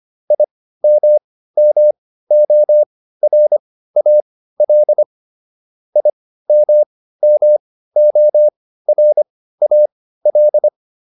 Morse code, heard as a keyed tone, about 25 wpm, spaced as for 15 wpm, 600 Hz. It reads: IMMORAL IMMORAL